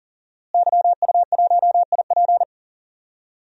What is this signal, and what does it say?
Morse code, heard as a keyed tone, about 40 wpm, 700 Hz. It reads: YU1IP
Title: DXCC-100 Call Signs - 5 Characters - Repeated 3x - 40wpm